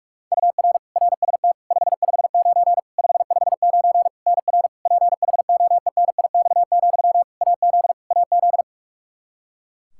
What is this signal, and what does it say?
Morse code, heard as a keyed tone, about 46 wpm, 700 Hz. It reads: UR RST 559 559 NR PHOENIX, AZ AZ
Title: Practice QSO - Exchange 1 - Beginning - 45wpm